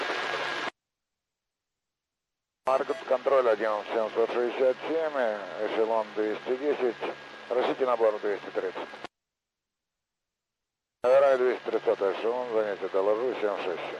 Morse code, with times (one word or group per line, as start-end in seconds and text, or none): none